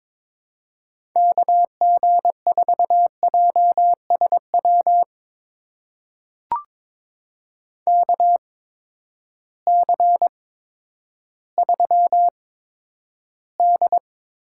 Morse code, heard as a keyed tone, about 22 wpm, spaced as for 11 wpm, 700 Hz. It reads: KG4JSW KC3D
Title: USA Call Signs - 6 Characters - 8x Character Spacing - 22wpm